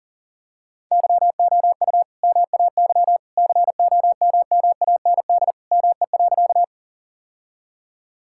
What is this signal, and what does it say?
Morse code, heard as a keyed tone, about 40 wpm, 700 Hz. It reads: YOU MAY COMMAND ME.
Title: Sentences from Top 1600 Words 40wpm